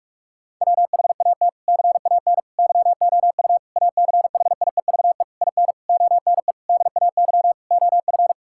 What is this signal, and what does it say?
Morse code, heard as a keyed tone, about 46 wpm, 700 Hz. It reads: WHAT CAN YOU ACHIEVE IN ONE DAY OF
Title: Sentences from Top 1000 Words - Review Part 6 - 45wpm